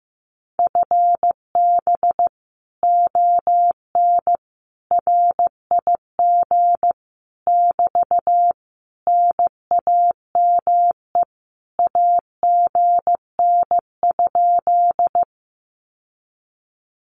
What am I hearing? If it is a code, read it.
Morse code, heard as a keyed tone, about 15 wpm, 700 Hz. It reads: FB ON RIG <BT> NAME AGN?
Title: Common QSO Phrases Encore 15wpm